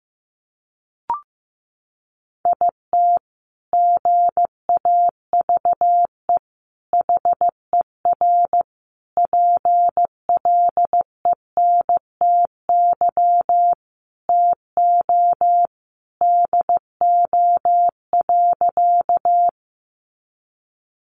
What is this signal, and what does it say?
Morse code, heard as a keyed tone, about 15 wpm, 700 Hz. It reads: IT GAVE HER PLENTY TO DO.
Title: Sentences from Top 1300 Words 15wpm